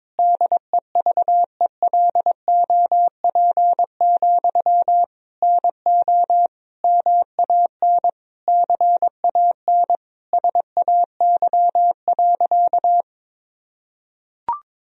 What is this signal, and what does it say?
Morse code, heard as a keyed tone, about 21 wpm, 700 Hz. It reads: DEVELOP, NO MAN CAN SAY.
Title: Sentences from Top 1000 Words - Review Part 2 - 22wpm